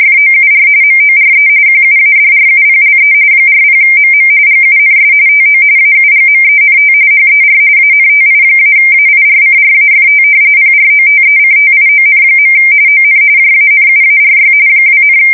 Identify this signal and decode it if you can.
RTTY, about 45.45 baud. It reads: FOUR MORE QUADRUPLE 40MM GUNS TO AUGMENT HER ANTI-AIRCRAFT PROTECTION. ON JANUARY 2,
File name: fsk170 / battleshipnj